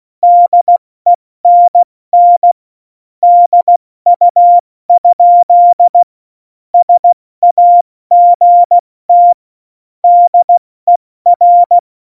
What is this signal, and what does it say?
Morse code, heard as a keyed tone, about 16 wpm, 700 Hz. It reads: DENN DU? SAGT DER
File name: Morsekurs / Witze / Witz 03. November 2020, 70 F80